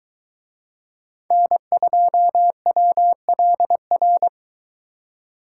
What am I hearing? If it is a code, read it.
Morse code, heard as a keyed tone, about 23 wpm, 700 Hz. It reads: N2WLR